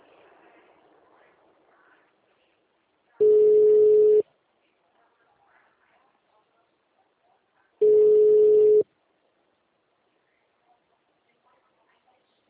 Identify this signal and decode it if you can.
Ringback tone (Europe, 425 Hz).
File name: out- -2017- - - .17